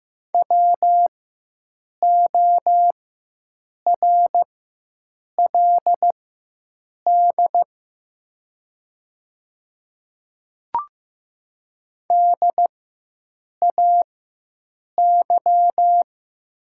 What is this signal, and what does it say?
Morse code, heard as a keyed tone, about 15 wpm, spaced as for 7 wpm, 700 Hz. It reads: WORLD DAY
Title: 4x Character Spacing - Top 200 Words - 15wpm